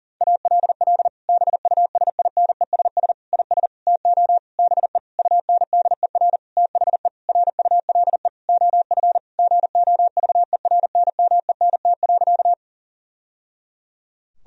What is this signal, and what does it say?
Morse code, heard as a keyed tone, about 40 wpm, 700 Hz. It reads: ALL BUSINESS IS TO BE UNDER THE RULE OF GOVERNMENT.
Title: Sentences from Top 1000 Words - Review Part 3 - 40wpm